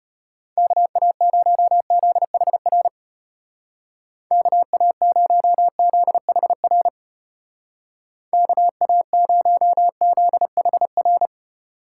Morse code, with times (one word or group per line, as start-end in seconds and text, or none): 0.6-2.9 KA0ZHR
4.3-6.9 KA0ZHR
8.3-11.3 KA0ZHR